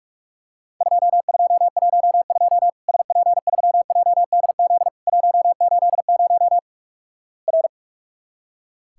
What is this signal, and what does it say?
Morse code, heard as a keyed tone, about 46 wpm, 700 Hz. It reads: 2212 SP3JDZ 180 R
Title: Worked All Europe DX Contest - QTC Exchange - 45wpm